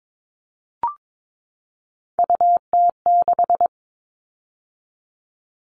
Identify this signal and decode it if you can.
Morse code, heard as a keyed tone, about 22 wpm, 700 Hz. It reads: UT6
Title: All Cut Numbers - 3-Digit 22wpm